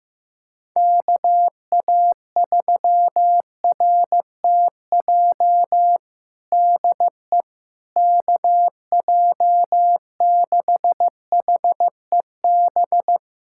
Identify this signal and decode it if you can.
Morse code, heard as a keyed tone, about 16 wpm, 700 Hz. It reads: KA3RTJ DE KJ6HEB
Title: USA Callsign DE Callsign 15wpm